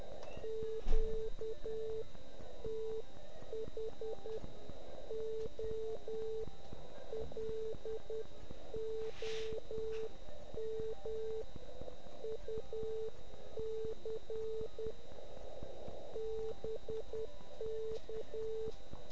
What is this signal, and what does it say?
Morse code, heard as a keyed tone, about 10 wpm, spaced as for 8 wpm, 440 Hz, with noise. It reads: QTH OLOMUC BK